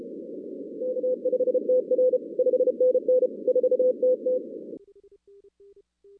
Morse code, heard as a keyed tone, about 30 wpm, 510 Hz. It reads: K5TR 5NN 4TT
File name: 2016 arrl dx cw / 2016-ARRL-DX-CW-K5TR- - 0-46